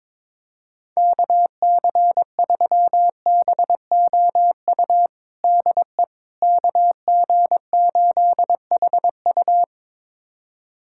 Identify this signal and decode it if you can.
Morse code, heard as a keyed tone, about 22 wpm, 700 Hz. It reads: KC3BOU DE KG8HU
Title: USA Callsign DE Callsign 22wpm